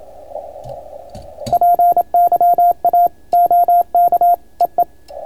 Morse code, heard as a keyed tone, about 27 wpm, 690 Hz. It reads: PYA OK EE